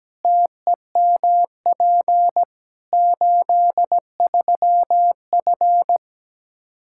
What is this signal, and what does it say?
Morse code, heard as a keyed tone, about 17 wpm, 700 Hz. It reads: TEMP 83F
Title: K1USN SST QSO Practice 17wpm